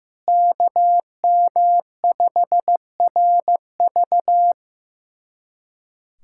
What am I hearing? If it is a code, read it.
Morse code, heard as a keyed tone, about 15 wpm, 700 Hz. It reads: KM5RV
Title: USA Callsign DE Callsign 15wpm